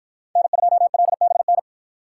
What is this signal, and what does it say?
Morse code, heard as a keyed tone, about 53 wpm, 700 Hz. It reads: N2LBN